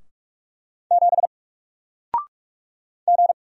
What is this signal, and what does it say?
Morse code, heard as a keyed tone, about 46 wpm, 700 Hz. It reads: 7 G
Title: Single Letter-Number Alt 45wpm